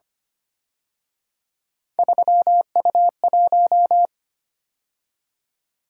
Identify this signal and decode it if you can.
Morse code, heard as a keyed tone, about 29 wpm, 700 Hz. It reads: 3U1
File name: Speed-Racing - Three Letter-Number Combinations - 25wpm